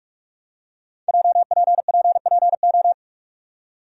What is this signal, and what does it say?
Morse code, heard as a keyed tone, about 44 wpm, 700 Hz. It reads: JPPPO
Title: Code Groups - 5 Letters - 45wpm